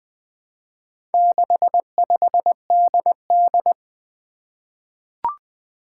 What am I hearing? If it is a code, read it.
Morse code, heard as a keyed tone, about 21 wpm, 700 Hz. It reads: 65DD
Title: Four Challenging Letter-Number Combinations 20wpm